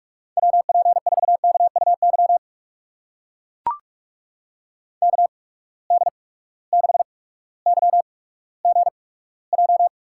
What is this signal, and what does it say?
Morse code, heard as a keyed tone, about 44 wpm, spaced as for 19 wpm, 700 Hz. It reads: WP4KUY KD6YGJ
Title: USA Call Signs - 8x Character Spacing - 45wpm